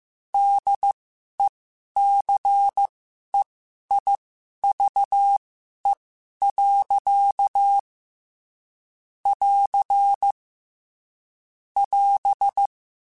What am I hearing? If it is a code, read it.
Morse code, heard as a keyed tone, about 15 wpm, spaced as for 11 wpm, 800 Hz. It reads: DECEIVE. <AR> <AS>